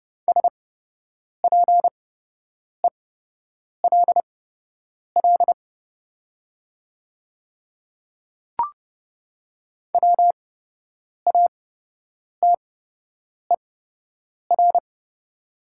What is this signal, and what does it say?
Morse code, heard as a keyed tone, about 30 wpm, spaced as for 8 wpm, 700 Hz. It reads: SPELL WATER
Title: 8x Character Spacing - Top 300 Words - 30wpm